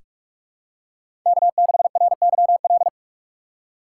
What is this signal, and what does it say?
Morse code, heard as a keyed tone, about 41 wpm, 700 Hz. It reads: K6RYL
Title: USA Call Signs - 5 Characters - 2x Character Spacing - 45wpm